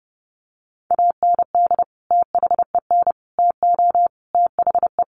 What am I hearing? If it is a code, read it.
Morse code, heard as a keyed tone, about 30 wpm, 700 Hz. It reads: AND THEN TO THE
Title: Sentences from Top 1000 Words 30wpm